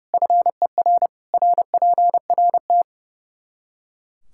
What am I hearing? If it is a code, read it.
Morse code, heard as a keyed tone, about 30 wpm, 700 Hz. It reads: FER RPRT
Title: K1USN SST QSO Practice 30wpm